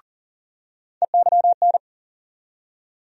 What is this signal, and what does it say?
Morse code, heard as a keyed tone, about 40 wpm, 700 Hz. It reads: EYN
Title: USA Call Sign Postfixes - 3 Letters 40wpm